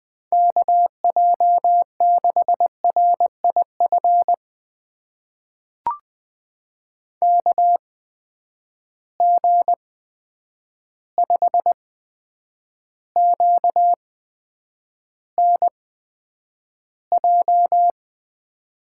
Morse code, heard as a keyed tone, about 20 wpm, spaced as for 9 wpm, 700 Hz. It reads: KJ6RIF KG5QNJ
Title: USA Call Signs - 8x Character Spacing - 20wpm